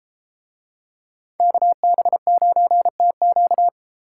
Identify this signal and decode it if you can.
Morse code, heard as a keyed tone, about 31 wpm, 700 Hz. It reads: KB9TQ